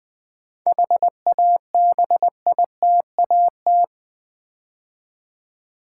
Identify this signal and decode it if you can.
Morse code, heard as a keyed tone, about 20 wpm, 700 Hz. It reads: HABITAT
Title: 7 Letter Words - First 3 Letters Given - 20wpm